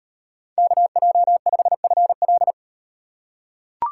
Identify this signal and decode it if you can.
Morse code, heard as a keyed tone, about 36 wpm, 700 Hz. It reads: KJ5FL